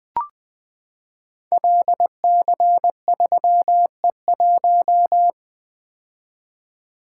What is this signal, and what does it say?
Morse code, heard as a keyed tone, about 20 wpm, 700 Hz. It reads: LC3E1
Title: Code Groups - 5 Letter:Numbers - 20wpm